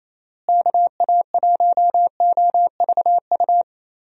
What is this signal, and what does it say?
Morse code, heard as a keyed tone, about 28 wpm, 700 Hz. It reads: KA1OVU
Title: USA Call Signs - 6 Characters - Repeated 3x - 28wpm